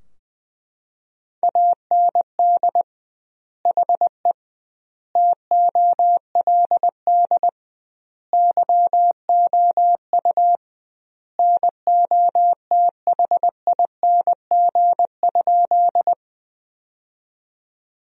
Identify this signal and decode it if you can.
Morse code, heard as a keyed tone, about 20 wpm, 700 Hz. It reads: AND HE TOLD YOU NOTHING?
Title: 2x Word-Spacing - Sentences from Top 500 Words - 20wpm